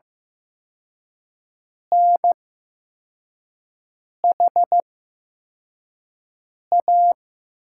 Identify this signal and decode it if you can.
Morse code, heard as a keyed tone, about 15 wpm, spaced as for 4 wpm, 700 Hz. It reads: NHA
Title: USA Call Signs - 5 Characters - 8x Character Spacing - 15wpm